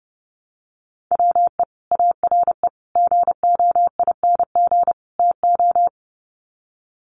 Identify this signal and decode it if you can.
Morse code, heard as a keyed tone, about 30 wpm, 700 Hz. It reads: WE ARE GOING TO
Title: Sets of 4 Words 30wpm